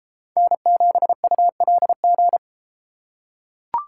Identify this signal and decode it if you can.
Morse code, heard as a keyed tone, about 32 wpm, 700 Hz. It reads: N7ULG